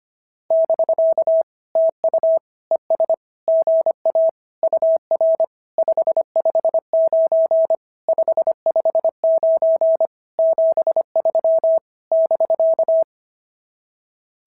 Morse code, heard as a keyed tone, about 25 wpm, 690 Hz. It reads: <BK> TU ES GA UR 559 559 73 <BK>
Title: SOTA QSO Practice 25wpm